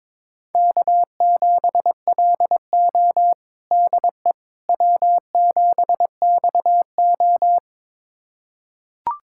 Morse code, heard as a keyed tone, about 22 wpm, 700 Hz. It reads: K7LO DE W7XO
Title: USA Callsign DE Callsign - 4 Characters - 22wpm